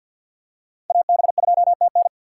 Morse code, heard as a keyed tone, about 49 wpm, 700 Hz. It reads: AB2TN
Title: USA Call Signs - 5 Characters - 4x Character Spacing - 50wpm